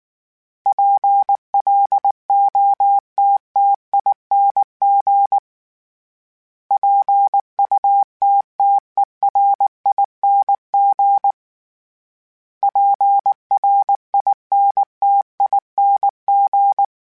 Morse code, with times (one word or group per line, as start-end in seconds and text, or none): 0.7-5.4 PLOTTING
6.7-11.3 PUTTERING
12.6-16.8 PRINTING